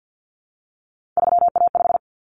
Morse code, heard as a keyed tone, about 50 wpm, 700 Hz. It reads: 3A5
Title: Common Cut Numbers - 3-Digit 50wpm